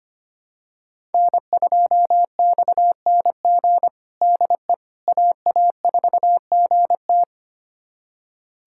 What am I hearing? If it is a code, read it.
Morse code, heard as a keyed tone, about 25 wpm, 700 Hz. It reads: N2XNG DE AA4GT
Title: USA Callsign DE Callsign - 5 Characters - 25wpm